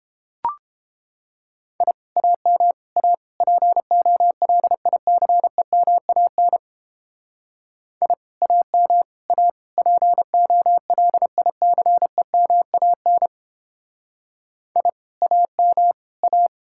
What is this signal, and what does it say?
Morse code, heard as a keyed tone, about 31 wpm, 700 Hz. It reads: I AM A POLICEMAN I AM A POLICEMAN I AM A